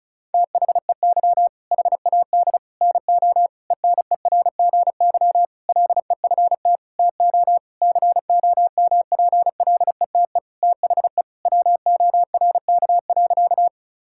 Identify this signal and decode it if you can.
Morse code, heard as a keyed tone, about 36 wpm, 700 Hz. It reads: THEY HAD NO ENERGY LEFT TO COMPLETE THE WORK.
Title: Sentences from Top 1000 Words - Review Part 6 - 35wpm